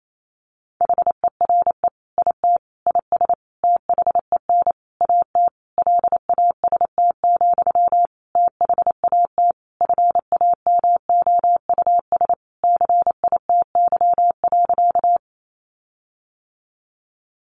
Morse code, heard as a keyed tone, about 28 wpm, 700 Hz. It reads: HERE IT IS THEN AT LAST, THAT FAMOUS CITY.
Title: Sentences from Top 1500 Words 28wpm